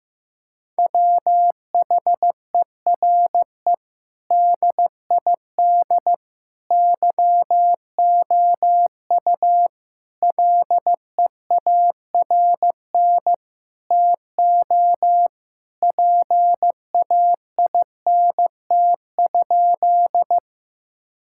Morse code, heard as a keyed tone, about 16 wpm, 700 Hz. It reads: WHERE DID YOU LEARN TO PAINT?
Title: Sentences from Top 1000 Words - Review Part 6 - 15wpm